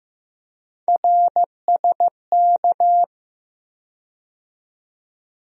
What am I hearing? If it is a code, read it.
Morse code, heard as a keyed tone, about 15 wpm, 700 Hz. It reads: RSK
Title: USA Call Sign Postfixes - 3 Letters 15wpm